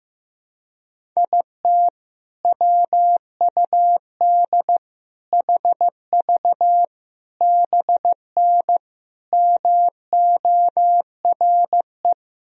Morse code, heard as a keyed tone, about 16 wpm, 700 Hz. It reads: IT WUD HV BN MORE
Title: Sentences from Top 400 Words and Abbreviations 15wpm